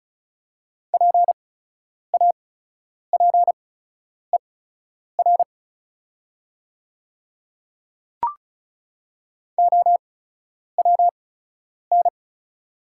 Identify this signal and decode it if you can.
Morse code, heard as a keyed tone, about 34 wpm, spaced as for 10 wpm, 700 Hz. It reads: PAPER OWN
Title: 8x Character Spacing - Top 300 Words - 35wpm